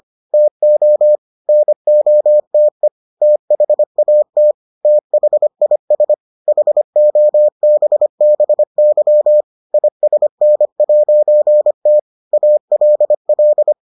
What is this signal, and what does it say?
Morse code, heard as a keyed tone, about 25 wpm, 600 Hz. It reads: TO NOTE THAT THIS HOBBY ISN'T ALL